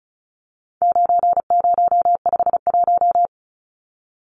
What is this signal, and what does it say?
Morse code, heard as a keyed tone, about 34 wpm, 700 Hz. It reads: 9051
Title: Numbers - 4-Digit 35wpm